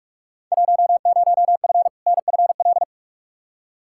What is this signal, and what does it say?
Morse code, heard as a keyed tone, about 44 wpm, 700 Hz. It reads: 10F NFL
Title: Field Day Exchanges Encore - 45wpm